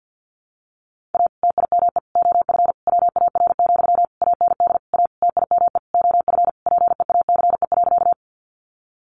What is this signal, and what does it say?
Morse code, heard as a keyed tone, about 50 wpm, 700 Hz. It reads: A TIME OF WAR, AND A TIME OF PEACE.
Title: Sentences from Top 1500 Words 50wpm